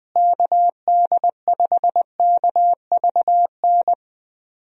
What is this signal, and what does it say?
Morse code, heard as a keyed tone, about 20 wpm, 700 Hz. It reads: KD5KVN
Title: USA Call Signs - 6 Characters - Repeated 3x - 20wpm